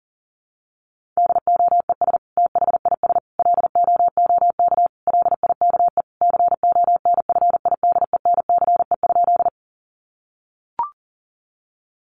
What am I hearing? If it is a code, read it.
Morse code, heard as a keyed tone, about 40 wpm, 700 Hz. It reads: DOES THIS LOOK LIKE CONFIDENCE?
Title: Sentences from Top 1500 Words 40wpm